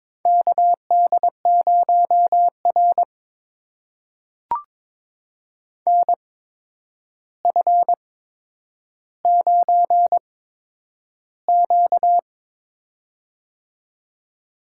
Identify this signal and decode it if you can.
Morse code, heard as a keyed tone, about 22 wpm, spaced as for 9 wpm, 700 Hz. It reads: KD0R NF9Q
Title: USA Call Signs - 4 Characters - 8x Character Spacing - 22wpm